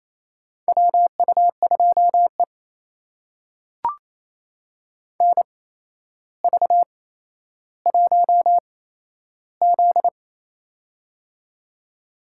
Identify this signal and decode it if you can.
Morse code, heard as a keyed tone, about 28 wpm, spaced as for 12 wpm, 700 Hz. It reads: WU2E NV1Z
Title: USA Call Signs - 4 Characters - 8x Character Spacing - 28wpm